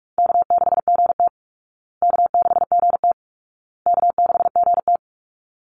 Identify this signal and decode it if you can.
Morse code, heard as a keyed tone, about 44 wpm, 700 Hz. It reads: K6GT K6GT K6GT
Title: Easy Call Signs Repeated 3x - 45wpm